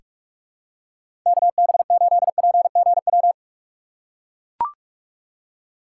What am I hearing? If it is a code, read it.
Morse code, heard as a keyed tone, about 44 wpm, 700 Hz. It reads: KB8PGW
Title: K1USN SST QSO Practice - 45wpm